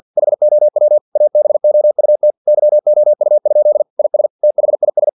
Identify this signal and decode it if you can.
Morse code, heard as a keyed tone, about 50 wpm, 600 Hz. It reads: HOW ABOUT YOU? IS THIS
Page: Never 49wpm